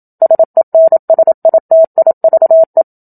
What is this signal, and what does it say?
Morse code, heard as a keyed tone, about 27 wpm, 650 Hz. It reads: SENSITIVE
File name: Book Two / 01-Chapter 1 - 27 WPM